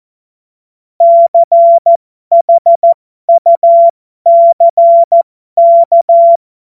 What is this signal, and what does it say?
Morse code, heard as a keyed tone, about 14 wpm, spaced as for 12 wpm, 680 Hz. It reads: CHUCK
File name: Morsekurs / Witze